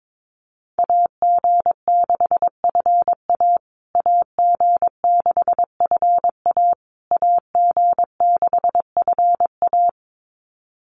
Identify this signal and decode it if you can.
Morse code, heard as a keyed tone, about 22 wpm, 690 Hz. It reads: AG6FA AG6FA AG6FA
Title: USA Call Signs Repeated 3x 22wpm